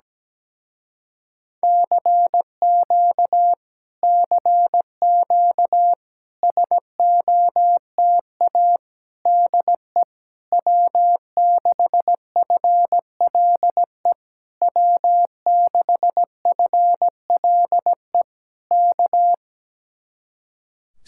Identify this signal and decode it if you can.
Morse code, heard as a keyed tone, about 17 wpm, 700 Hz. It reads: CQ CQ SOTA DE W6FLE W6FLE K
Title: Speed-Racing Common QSO Phrases Encore - 17wpm